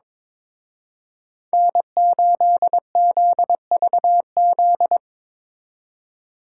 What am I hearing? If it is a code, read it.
Morse code, heard as a keyed tone, about 24 wpm, 700 Hz. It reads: N8ZVZ
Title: Speed Racing - USA 1x3 Call Signs 22wpm